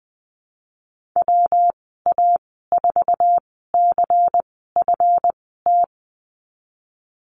Morse code, heard as a keyed tone, about 20 wpm, spaced as for 15 wpm, 700 Hz. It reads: WA4CFT
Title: 200 Lesson 32g - Letter k - Call Signs 20wpm